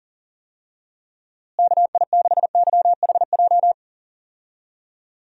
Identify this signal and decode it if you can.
Morse code, heard as a keyed tone, about 38 wpm, 700 Hz. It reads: KI6YHJ
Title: USA Call Signs - 6 Characters - Repeated 2x - 40wpm